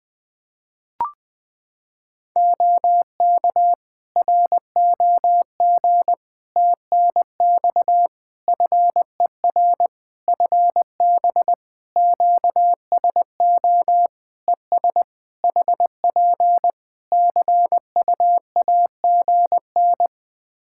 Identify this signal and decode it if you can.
Morse code, heard as a keyed tone, about 20 wpm, 700 Hz. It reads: OK ROG TNX FER FB QSO ES HP CUAGN
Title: Practice QSO - Exchange 4 - Ending - 20wpm